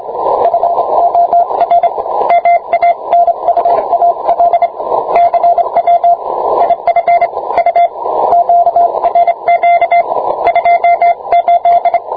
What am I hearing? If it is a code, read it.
Morse code, heard as a keyed tone, about 25 wpm, 690 Hz, with radio noise. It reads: FOR MANUAL CW IF U QRQ 28